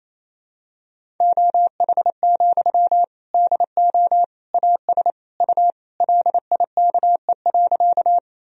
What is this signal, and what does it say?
Morse code, heard as a keyed tone, about 28 wpm, 700 Hz. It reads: OH, DO AS U LIKE.
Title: Sentences from Top 500 Words and Abbreviations 28wpm